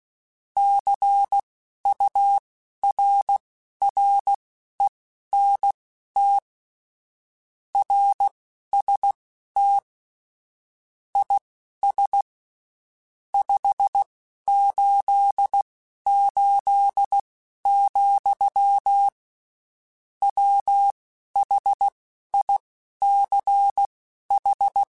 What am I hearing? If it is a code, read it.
Morse code, heard as a keyed tone, about 16 wpm, spaced as for 11 wpm, 800 Hz. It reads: CURRENT RST IS 588, WHICH
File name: (spaced) / QSOs Spaced-mp3s / WA7QOX 1 @15 sp